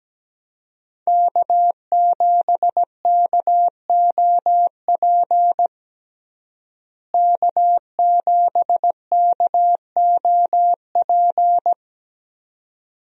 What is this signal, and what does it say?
Morse code, heard as a keyed tone, about 17 wpm, 700 Hz. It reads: K7KOP K7KOP
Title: USA Call Signs - 5 Characters - Repeated 2x - 17wpm